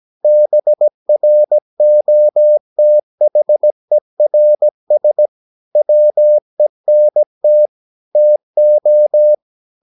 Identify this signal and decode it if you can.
Morse code, heard as a keyed tone, about 17 wpm, 600 Hz. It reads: BROTHERS WENT TO
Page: Romanen Never af Ken Follet på engelsk.